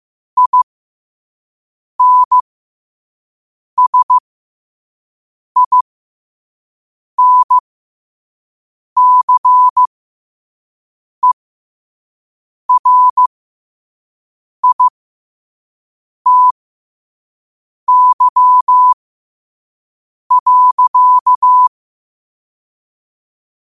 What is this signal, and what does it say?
Morse code, heard as a keyed tone, about 15 wpm, spaced as for 5 wpm, 1000 Hz. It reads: INSINCERITY.
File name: qotdfiles / 5wpm